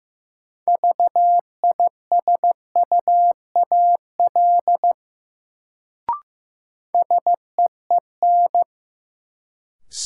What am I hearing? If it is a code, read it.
Morse code, heard as a keyed tone, about 15 wpm, 700 Hz. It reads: VISUAL SEEN
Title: Sets of 5 Words Encore - New Word Drill - 15wpm